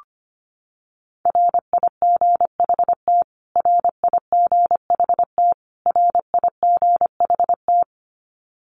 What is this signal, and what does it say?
Morse code, heard as a keyed tone, about 25 wpm, 700 Hz. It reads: RIGHT RIGHT RIGHT
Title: Top 500 Words Repeated 25wpm